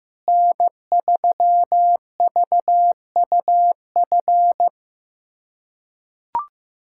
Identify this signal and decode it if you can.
Morse code, heard as a keyed tone, about 15 wpm, 700 Hz. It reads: N3VUF
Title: USA Callsign DE Callsign - 5 Characters - 15wpm